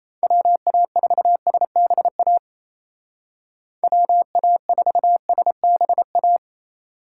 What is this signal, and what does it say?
Morse code, heard as a keyed tone, about 37 wpm, 700 Hz. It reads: WA4SBA WA4SBA